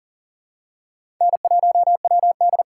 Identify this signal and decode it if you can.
Morse code, heard as a keyed tone, about 38 wpm, 700 Hz. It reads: N1WD
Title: USA Call Signs - 4 Characters - 3x Character Spacing - 40wpm